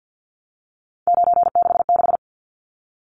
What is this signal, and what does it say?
Morse code, heard as a keyed tone, about 50 wpm, 700 Hz. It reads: 966
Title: Numbers - 3-Digit 50wpm